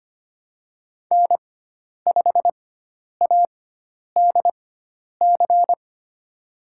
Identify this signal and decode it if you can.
Morse code, heard as a keyed tone, about 25 wpm, spaced as for 10 wpm, 700 Hz. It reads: N5ADC
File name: Lesson 16g - Letter c - Call Signs 25wpm